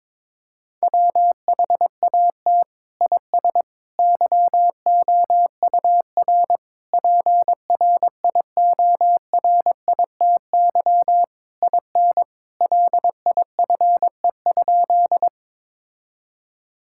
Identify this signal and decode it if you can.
Morse code, heard as a keyed tone, about 22 wpm, 700 Hz. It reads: WHAT IS YOUR PRIORITY IN LIFE?
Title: Sentences from Top 1600 Words 22wpm